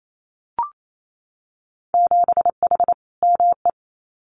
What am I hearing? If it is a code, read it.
Morse code, heard as a keyed tone, about 28 wpm, 700 Hz. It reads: 7H ME
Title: Winter Field Day Exchanges Encore 28wpm